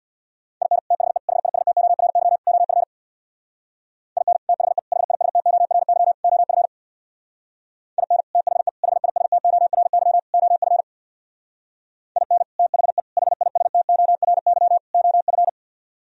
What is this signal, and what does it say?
Morse code, heard as a keyed tone, about 64 wpm, 700 Hz. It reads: IN THE HISTORY OF IN THE HISTORY OF IN THE HISTORY OF IN THE HISTORY OF